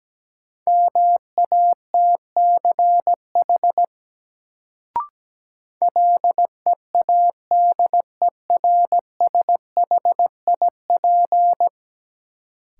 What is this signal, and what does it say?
Morse code, heard as a keyed tone, about 17 wpm, 700 Hz. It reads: MATCH LEADERSHIP
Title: Sentences from Top 1600 Words - New Word Drill - 17wpm